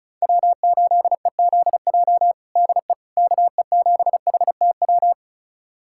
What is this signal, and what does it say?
Morse code, heard as a keyed tone, about 36 wpm, 700 Hz. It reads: W8EZJ DE KE7HTW
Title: USA Callsign DE Callsign 35wpm